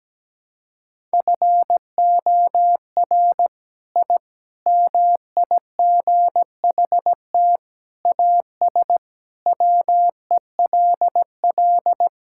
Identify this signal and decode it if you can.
Morse code, heard as a keyed tone, about 17 wpm, 700 Hz. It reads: FOR I MIGHT AS WELL
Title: Sentences from Top 1000 Words - Review Part 5 - 17wpm